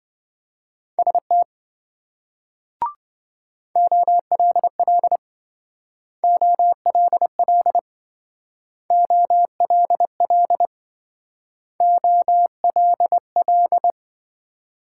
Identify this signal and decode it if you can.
Morse code, heard as a keyed tone, about 27 wpm, 700 Hz. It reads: ST OLL OLL OLL OLL